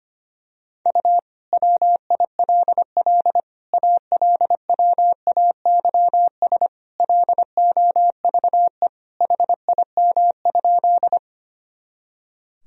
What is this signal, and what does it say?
Morse code, heard as a keyed tone, about 25 wpm, 700 Hz. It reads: U WILL ALWAYS LOVE HIM?
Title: Sentences from Top 400 Words and Abbreviations 25wpm